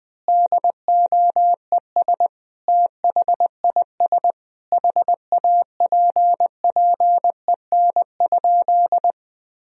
Morse code, heard as a keyed tone, about 21 wpm, 700 Hz. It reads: DOES THIS HAPPEN?
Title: Sentences from Top 400 Words and Abbreviations 20wpm